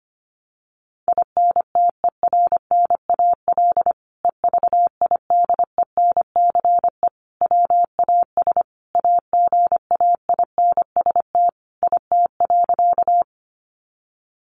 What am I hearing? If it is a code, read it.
Morse code, heard as a keyed tone, about 25 wpm, 700 Hz. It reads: INTERNAL EVIDENCE WAS AGAINST IT.
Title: Sentences from Top 1400 Words 25wpm